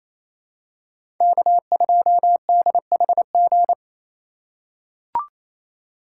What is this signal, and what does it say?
Morse code, heard as a keyed tone, about 28 wpm, 700 Hz. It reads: K2DHG
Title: USA Call Signs - 5 Characters - 4x Character Spacing - 28wpm